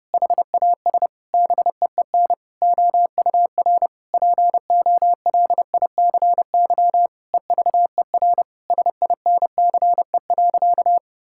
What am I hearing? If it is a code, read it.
Morse code, heard as a keyed tone, about 30 wpm, 700 Hz. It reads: HAS BEEN OUR POLICY EVER SINCE.
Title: Sentences from Top 1000 Words - Review Part 5 - 30wpm